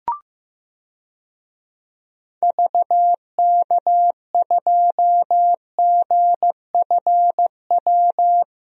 Morse code, heard as a keyed tone, about 15 wpm, 700 Hz. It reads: VK2GFW
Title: DXCC-100 Call Signs - 6 Characters - Repeated 2x - 15wpm